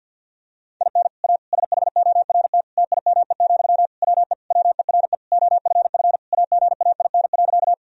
Morse code, heard as a keyed tone, about 50 wpm, 700 Hz. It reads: IN A SHORT TIME, WE WERE OFF AGAIN.
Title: Sentences from Top 1000 Words - Review Part 2 - 50wpm